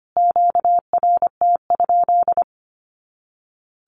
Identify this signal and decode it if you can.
Morse code, heard as a keyed tone, about 25 wpm, 700 Hz. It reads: QRT?
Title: Q-Codes Repeated 3x 25wpm